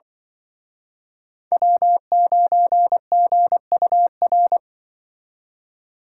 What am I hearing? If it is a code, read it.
Morse code, heard as a keyed tone, about 24 wpm, 700 Hz. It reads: W9GUR
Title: Speed Racing - USA 1x3 Call Signs 20wpm